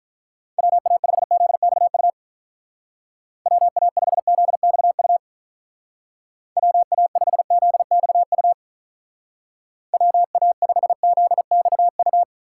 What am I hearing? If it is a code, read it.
Morse code, heard as a keyed tone, about 45 wpm, 700 Hz. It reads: WA5ZXU WA5ZXU WA5ZXU WA5ZXU